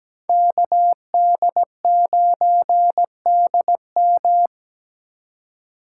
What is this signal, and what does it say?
Morse code, heard as a keyed tone, about 20 wpm, 700 Hz. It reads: KD9DM
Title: Speed Racing - USA 2x2 Call Signs 17wpm